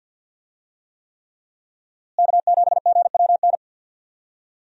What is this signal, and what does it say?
Morse code, heard as a keyed tone, about 50 wpm, 700 Hz. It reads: K7GWN
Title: USA Call Signs - 5 Characters - Repeated 3x - 50wpm